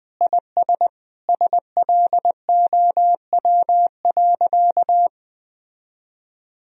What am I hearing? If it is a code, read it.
Morse code, heard as a keyed tone, about 20 wpm, 700 Hz. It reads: IS SLOW.
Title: Sentences from Top 400 Words and Abbreviations 20wpm